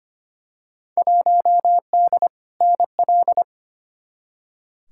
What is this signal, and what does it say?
Morse code, heard as a keyed tone, about 25 wpm, 700 Hz. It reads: 1D NL
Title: Field Day Exchanges Encore - 25wpm